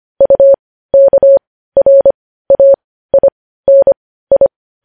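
Morse code, heard as a keyed tone, about 25 wpm, spaced as for 15 wpm, 550 Hz. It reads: UKRAINI
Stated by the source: Morse practice sending news headlines at 15 wpm.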